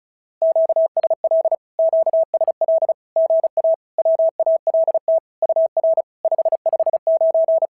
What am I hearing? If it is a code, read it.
Morse code, heard as a keyed tone, about 36 wpm, 650 Hz. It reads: QSL QSL GA WALT UR 559
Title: SOTA QSO Practice 35wpm